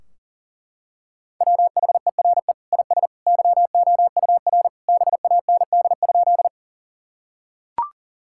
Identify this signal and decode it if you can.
Morse code, heard as a keyed tone, about 40 wpm, 700 Hz. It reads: WHERE IS YOUR BAND?
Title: Sentences from Top 1600 Words 40wpm